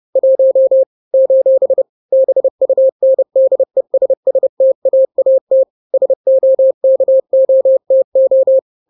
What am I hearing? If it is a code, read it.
Morse code, heard as a keyed tone, about 30 wpm, 530 Hz. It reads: 1 : BUNDESSTAAT SOKOTO
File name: daily / 29wpm